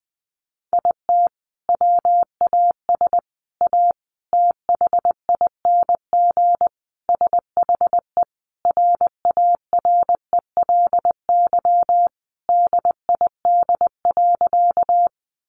Morse code, heard as a keyed tone, about 19 wpm, 700 Hz. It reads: IT WAS A THING SHE RARELY DID.
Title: Sentences from Top 1400 Words 20wpm